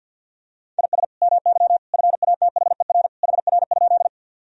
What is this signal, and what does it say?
Morse code, heard as a keyed tone, about 50 wpm, 700 Hz. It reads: IS MY FATHER HR?
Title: Sentences from Top 300 Words and Abbreviations 50wpm